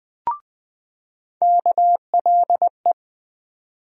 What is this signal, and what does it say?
Morse code, heard as a keyed tone, about 20 wpm, 700 Hz. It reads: KLE
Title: USA Call Sign Postfixes - 3 Letters 20wpm